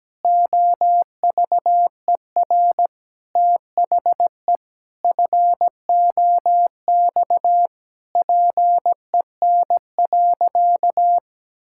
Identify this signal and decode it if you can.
Morse code, heard as a keyed tone, about 17 wpm, 700 Hz. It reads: OVER THE FOX PEN.
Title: Pangram Challenge v2 - 17wpm